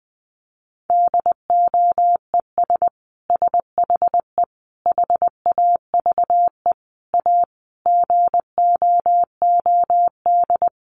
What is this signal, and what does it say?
Morse code, heard as a keyed tone, about 20 wpm, 700 Hz. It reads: DOES SHE HAVE A GOOD
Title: Sentences from Top 1400 Words 20wpm